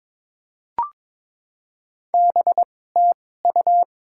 Morse code, heard as a keyed tone, about 22 wpm, spaced as for 16 wpm, 700 Hz. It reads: BTU
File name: Top 20 QSO Elements - 2x Character Spacing - 22wpm